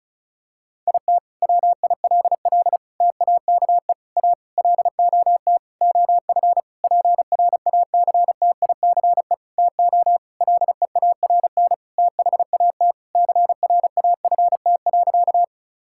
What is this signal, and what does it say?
Morse code, heard as a keyed tone, about 36 wpm, 700 Hz. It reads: IT WILL TAKE A LOT OF PRACTICE TO LEARN THAT CRAFT.
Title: Sentences from Top 1000 Words - Review Part 6 - 35wpm